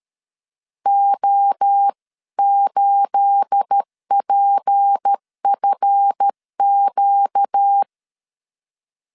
Morse code, heard as a keyed tone, about 13 wpm, 800 Hz. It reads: O8PFQ
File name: CW / DET / 1 9 1500C S B 12wpm 3819